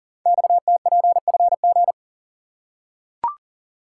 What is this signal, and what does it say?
Morse code, heard as a keyed tone, about 38 wpm, 700 Hz. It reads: XTPFG
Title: Code Groups - 5 Letters - 40wpm